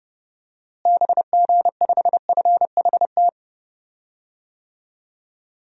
Morse code, heard as a keyed tone, about 30 wpm, 700 Hz. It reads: BG5FHT
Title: DXCC-100 Call Signs - 6 Characters - Repeated 3x - 30wpm